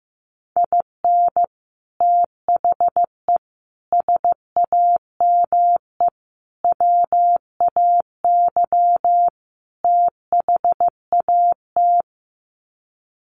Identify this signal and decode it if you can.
Morse code, heard as a keyed tone, about 15 wpm, 700 Hz. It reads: IN THE SAME WAY THAT
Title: Sets of 5 Words 15wpm